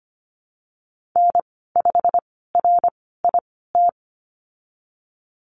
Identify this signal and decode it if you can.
Morse code, heard as a keyed tone, about 25 wpm, spaced as for 16 wpm, 700 Hz. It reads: N5RIT
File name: Lesson 15g - Number 5 - Call Signs 25wpm